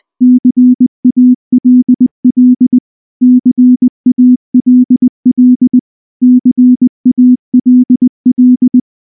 Morse code, heard as a keyed tone, about 19 wpm, 250 Hz. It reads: CALL CALL CALL
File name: morse / morse-training-20WPM-6